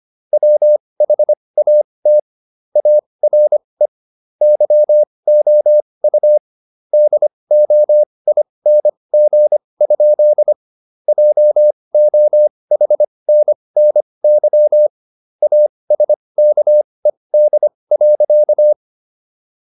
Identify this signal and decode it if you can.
Morse code, heard as a keyed tone, about 25 wpm, spaced as for 20 wpm, 600 Hz. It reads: WHAT ARE YOU DOING? JOHNNY ASKED.